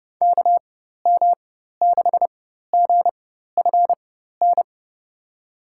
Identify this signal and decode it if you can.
Morse code, heard as a keyed tone, about 31 wpm, spaced as for 14 wpm, 700 Hz. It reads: KM6GFN
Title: USA Call Signs - 6 Characters - 4x Character Spacing - 30wpm